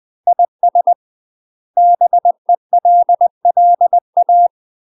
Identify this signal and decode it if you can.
Morse code, heard as a keyed tone, about 20 wpm, 700 Hz. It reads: IS BELLA
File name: UK MASTER QSOs TRAINING / 20WPM / UK MASTER QSOs 0008